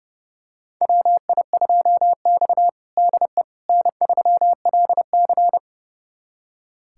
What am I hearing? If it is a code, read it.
Morse code, heard as a keyed tone, about 32 wpm, 700 Hz. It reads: WI2X DE N3LC
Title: USA Callsign DE Callsign - 4 Characters - 30wpm